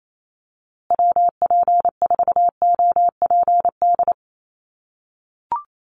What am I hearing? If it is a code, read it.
Morse code, heard as a keyed tone, about 28 wpm, 700 Hz. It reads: WP4OPD
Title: USA Call Signs Repeated 2x 28wpm